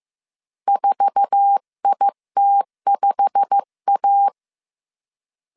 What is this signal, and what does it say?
Morse code, heard as a keyed tone, about 15 wpm, 800 Hz. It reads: 4IT5A
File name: CW / DET / 1 9 1500C S B 14wpm 3302